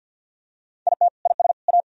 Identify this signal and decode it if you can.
Morse code, heard as a keyed tone, about 50 wpm, 700 Hz. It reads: IT IS A
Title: Sentences from Top 1000 Words - Review Part 3 - 50wpm